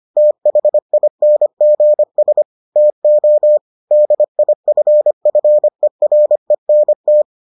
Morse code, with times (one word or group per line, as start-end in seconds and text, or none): 0.2-2.4 THINGS
2.8-3.6 TO
3.9-7.2 DIFFERENT